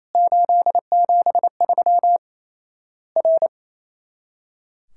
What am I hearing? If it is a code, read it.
Morse code, heard as a keyed tone, about 27 wpm, 700 Hz. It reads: 873 R
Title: Worked All Europe DX Contest - QTC Exchange - 28wpm